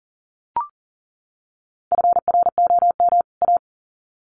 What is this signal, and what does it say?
Morse code, heard as a keyed tone, about 40 wpm, 700 Hz. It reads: FROM A
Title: Sets of 2 Words 40wpm